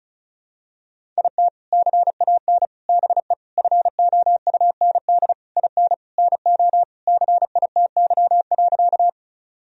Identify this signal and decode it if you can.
Morse code, heard as a keyed tone, about 36 wpm, 700 Hz. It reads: IT CAN BE FOUND IN NO CITY.
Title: Sentences from Top 1000 Words - Review Part 1 - 35wpm